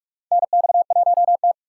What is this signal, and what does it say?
Morse code, heard as a keyed tone, about 46 wpm, 700 Hz. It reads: NX1T
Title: USA Call Signs - 4 Characters - 6x Character Spacing - 45wpm